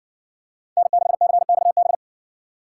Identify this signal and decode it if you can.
Morse code, heard as a keyed tone, about 60 wpm, 700 Hz. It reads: N6CXB